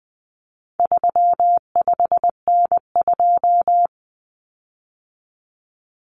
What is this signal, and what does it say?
Morse code, heard as a keyed tone, about 20 wpm, 700 Hz. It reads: 35N2
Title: Common Cut Numbers - 4-Digit 20wpm